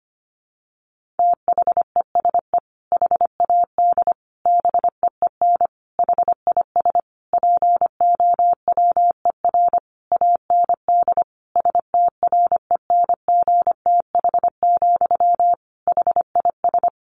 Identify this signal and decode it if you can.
Morse code, heard as a keyed tone, about 25 wpm, 700 Hz. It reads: THESE HAD BEEN HIS POWER AND STRENGTH, HIS
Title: Sentences from Top 1500 Words 25wpm